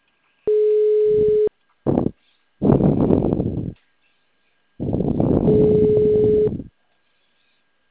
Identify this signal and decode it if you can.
Ringback tone (Europe, 425 Hz).